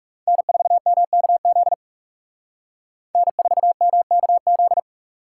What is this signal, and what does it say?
Morse code, heard as a keyed tone, about 38 wpm, 700 Hz. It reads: N4MKZ N4MKZ